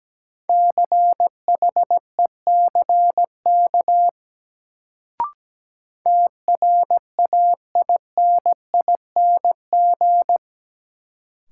Morse code, heard as a keyed tone, about 17 wpm, 700 Hz. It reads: CHECK TRAINING
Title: Sentences from Top 1400 Words - New Word Drill Alt - 17wpm